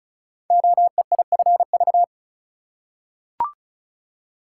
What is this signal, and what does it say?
Morse code, heard as a keyed tone, about 36 wpm, 700 Hz. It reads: OEIFV
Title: Code Groups - 5 Letters - 35wpm